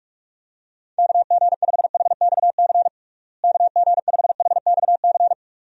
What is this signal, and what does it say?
Morse code, heard as a keyed tone, about 46 wpm, 700 Hz. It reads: KG5HXC KG5HXC
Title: USA Call Signs - 6 Characters - Repeated 3x - 45wpm